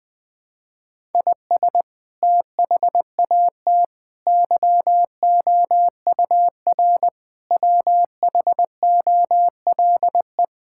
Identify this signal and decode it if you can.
Morse code, heard as a keyed tone, about 21 wpm, 700 Hz. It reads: IS THAT YOUR WHOLE
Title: Sentences from Top 1000 Words - Review Part 5 - 20wpm